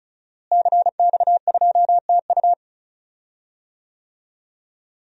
Morse code, heard as a keyed tone, about 34 wpm, 700 Hz. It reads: CX2TU
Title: DXCC-100 Call Signs - 5 Characters - Repeated 2x - 35wpm